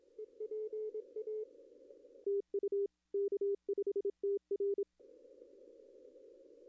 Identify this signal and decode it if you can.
Morse code, heard as a keyed tone, about 26 wpm, 410 Hz. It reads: EPA TU K5TR
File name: contests / arrl-160-k5tr-op-w5tn -0800